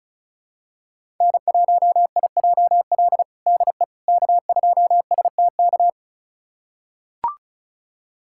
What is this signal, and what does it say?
Morse code, heard as a keyed tone, about 36 wpm, 700 Hz. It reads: N1IJL DE K2STK
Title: USA Callsign DE Callsign - 5 Characters - 35wpm